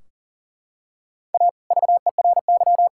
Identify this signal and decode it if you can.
Morse code, heard as a keyed tone, about 37 wpm, 700 Hz. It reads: A VERY
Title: Sentences from Top 1600 Words 40wpm